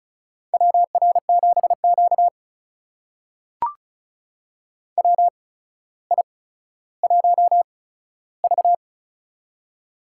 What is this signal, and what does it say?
Morse code, heard as a keyed tone, about 36 wpm, spaced as for 15 wpm, 700 Hz. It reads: WR7Q WI1V
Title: USA Call Signs - 4 Characters - 8x Character Spacing - 35wpm